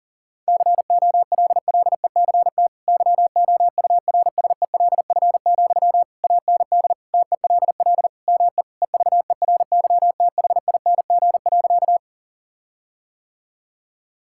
Morse code, heard as a keyed tone, about 37 wpm, 700 Hz. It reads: COLLECT YOURSELF, AND TELL ME EVERYTHING.
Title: Sentences from Top 1600 Words 40wpm